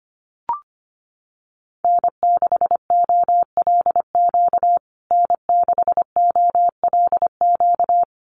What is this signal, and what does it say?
Morse code, heard as a keyed tone, about 25 wpm, 700 Hz. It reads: N6OLQ N6OLQ
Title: USA Call Signs Repeated 3x 25wpm